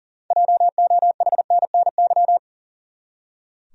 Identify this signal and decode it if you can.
Morse code, heard as a keyed tone, about 40 wpm, 700 Hz. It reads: JOHNNY
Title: USA Common Names - 40wpm